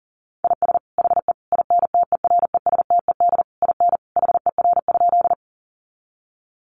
Morse code, heard as a keyed tone, about 38 wpm, 700 Hz. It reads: IS HE INTERESTED IN HER?
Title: Sentences from Top 900 Words 40wpm